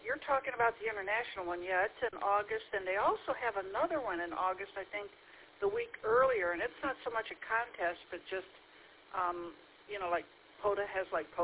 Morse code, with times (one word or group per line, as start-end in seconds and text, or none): none